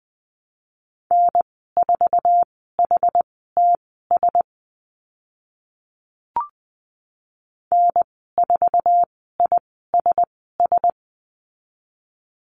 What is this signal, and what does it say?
Morse code, heard as a keyed tone, about 20 wpm, spaced as for 14 wpm, 700 Hz. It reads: N4HTS N4ISS
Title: 062 Lesson 11g - Letter h - Call Signs 20wpm